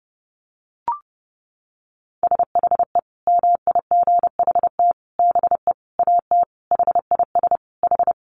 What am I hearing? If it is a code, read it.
Morse code, heard as a keyed tone, about 30 wpm, 700 Hz. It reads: SHE MIGHT BE AT HIS H
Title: Sentences from Top 300 Words 30wpm